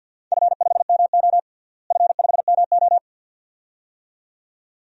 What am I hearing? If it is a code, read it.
Morse code, heard as a keyed tone, about 52 wpm, 700 Hz. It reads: F5MO F5MO
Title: DXCC-100 Call Signs - 4 Characters - Repeated 2x - 50wpm